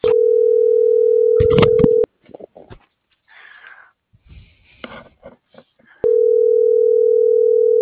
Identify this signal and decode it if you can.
Ringback tone (North America), 440 + 480 Hz.